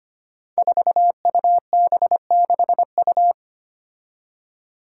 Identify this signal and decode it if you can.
Morse code, heard as a keyed tone, about 25 wpm, 700 Hz. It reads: 4UB6U
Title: Five Challenging Letter-Number Combinations 25wpm